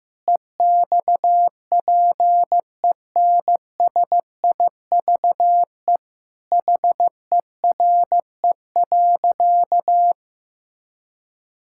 Morse code, heard as a keyed tone, about 15 wpm, 700 Hz. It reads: EXPENSIVE HERE.
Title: Sentences from Top 1000 Words - Review Part 6 - 15wpm